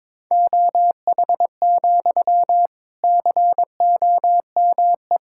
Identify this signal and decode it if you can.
Morse code, heard as a keyed tone, about 22 wpm, 700 Hz. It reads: OH, COME
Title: Sentences from Top 1600 Words 22wpm